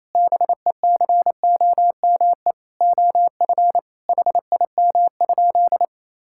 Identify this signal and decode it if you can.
Morse code, heard as a keyed tone, about 28 wpm, 700 Hz. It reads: BECOME OF HIM?
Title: Sentences from Top 1000 Words - Review Part 5 - 28wpm